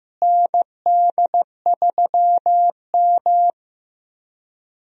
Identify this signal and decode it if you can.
Morse code, heard as a keyed tone, about 15 wpm, 700 Hz. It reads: ND3M
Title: USA Call Signs - 4 Characters - Repeated 3x - 15wpm